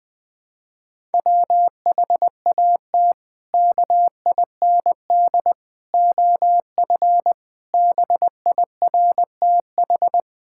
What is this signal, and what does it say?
Morse code, heard as a keyed tone, about 20 wpm, 700 Hz. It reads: WHAT KIND OF BIRTH
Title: Sentences from Top 1000 Words - Review Part 5 - 20wpm